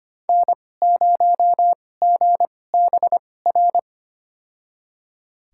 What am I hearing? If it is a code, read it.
Morse code, heard as a keyed tone, about 25 wpm, spaced as for 18 wpm, 700 Hz. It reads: N0GBR
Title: USA Call Signs - 5 Characters - 2x Character Spacing - 25wpm